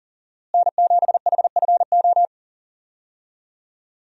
Morse code, heard as a keyed tone, about 40 wpm, 700 Hz. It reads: N7HFO
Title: USA Call Signs - 5 Characters - Repeated 3x - 40wpm